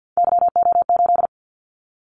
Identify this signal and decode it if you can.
Morse code, heard as a keyed tone, about 50 wpm, 700 Hz. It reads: YO8
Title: Callsign Prefixes for Top 25 DXCC - 50wpm